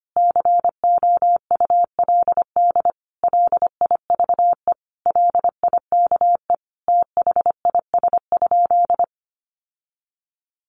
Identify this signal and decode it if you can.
Morse code, heard as a keyed tone, about 25 wpm, 700 Hz. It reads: COULD LIVE LIKE THIS?
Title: Sentences from Top 1400 Words 25wpm